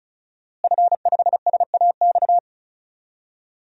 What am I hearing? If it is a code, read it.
Morse code, heard as a keyed tone, about 34 wpm, 700 Hz. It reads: F5SAX
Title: DXCC-100 Call Signs - 5 Characters - Repeated 3x - 35wpm